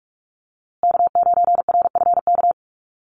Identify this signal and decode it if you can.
Morse code, heard as a keyed tone, about 44 wpm, 700 Hz. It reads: K9RFK
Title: USA Call Signs Repeated 3x 45wpm